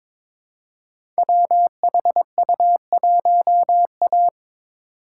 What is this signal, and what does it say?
Morse code, heard as a keyed tone, about 22 wpm, 700 Hz. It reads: WHU1A
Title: Code Groups - 5 Letter:Numbers - 22wpm